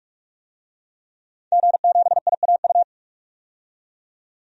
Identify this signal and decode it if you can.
Morse code, heard as a keyed tone, about 46 wpm, 700 Hz. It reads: G7IAU
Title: DXCC-100 Call Signs - 5 Characters - 45wpm